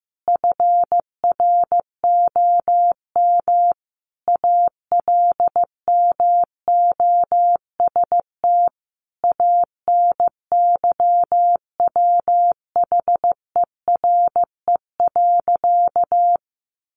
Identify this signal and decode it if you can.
Morse code, heard as a keyed tone, about 15 wpm, 700 Hz. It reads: FROM ALMOST ANYWHERE.
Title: Sentences from Top 1400 Words 15wpm